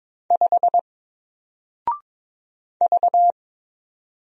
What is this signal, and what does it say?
Morse code, heard as a keyed tone, about 22 wpm, 700 Hz. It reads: 5 V
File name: Single-Letter-Number-22wpm